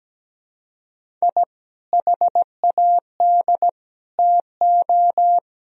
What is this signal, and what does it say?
Morse code, heard as a keyed tone, about 17 wpm, 700 Hz. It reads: I HAD TO
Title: Sentences from Top 1000 Words - Review Part 4 - 17wpm